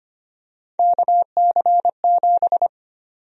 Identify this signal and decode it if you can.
Morse code, heard as a keyed tone, about 25 wpm, 700 Hz. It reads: KC7
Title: USA Call Sign Prefixes - 25wpm